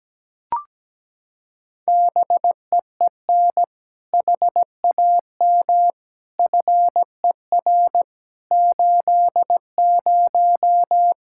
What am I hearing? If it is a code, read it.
Morse code, heard as a keyed tone, about 17 wpm, 700 Hz. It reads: BEEN HAM FER 80
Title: K1USN SST QSO Practice 17wpm